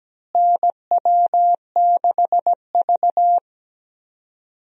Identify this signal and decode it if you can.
Morse code, heard as a keyed tone, about 17 wpm, 700 Hz. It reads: NW6V
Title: USA Call Signs - 4 Characters - Repeated 3x - 17wpm